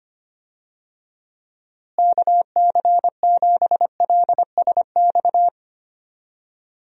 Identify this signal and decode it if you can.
Morse code, heard as a keyed tone, about 25 wpm, 700 Hz. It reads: KC7LSX
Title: USA Call Signs - 6 Characters - Repeated 3x - 25wpm